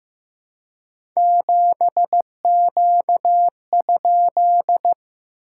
Morse code, heard as a keyed tone, about 15 wpm, 700 Hz. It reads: 7Q?
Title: SOTA QSO Practice 15wpm